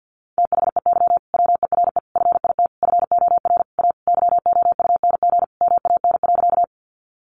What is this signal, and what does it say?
Morse code, heard as a keyed tone, about 50 wpm, 700 Hz. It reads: THEY WERE FIT FOR A YOUNG MAN.
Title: Sentences from Top 800 Words 50wpm